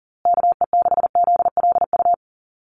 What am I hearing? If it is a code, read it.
Morse code, heard as a keyed tone, about 40 wpm, 700 Hz. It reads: KE6ZLU
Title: USA Call Signs Repeated 2x 40wpm